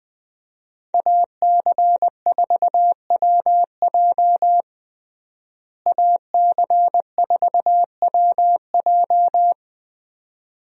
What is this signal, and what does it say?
Morse code, heard as a keyed tone, about 20 wpm, 700 Hz. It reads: AC4WJ AC4WJ
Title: USA Call Signs - 5 Characters - Repeated 2x - 20wpm